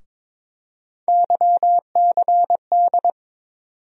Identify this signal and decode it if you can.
Morse code, heard as a keyed tone, about 22 wpm, 700 Hz. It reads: YCD
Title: USA Call Sign Postfixes - 3 Letters 22wpm